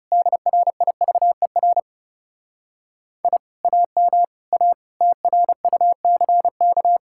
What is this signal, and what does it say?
Morse code, heard as a keyed tone, about 37 wpm, 700 Hz. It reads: DRIVER I AM A TRUCK
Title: Speed-Racing Common QSO Phrases - 30wpm